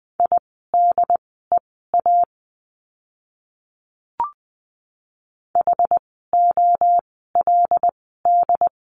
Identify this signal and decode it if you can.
Morse code, heard as a keyed tone, about 20 wpm, spaced as for 14 wpm, 700 Hz. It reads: IDEA HOLD
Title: 108 Lesson 18f - Letter m - Words 20wpm